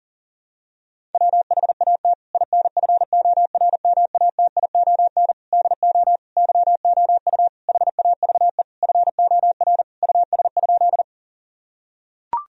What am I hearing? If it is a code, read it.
Morse code, heard as a keyed tone, about 37 wpm, 700 Hz. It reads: WHAT INFORMATION DO YOU HAVE FOR US?
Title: Sentences from Top 1000 Words - Review Part 3 - 40wpm